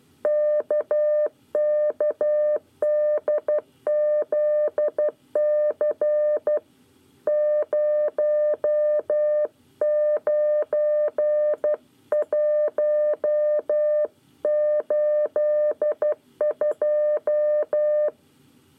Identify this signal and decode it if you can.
Morse code, heard as a keyed tone, about 12 wpm, 600 Hz. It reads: KKDZC 09182
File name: 9 lezione cw 1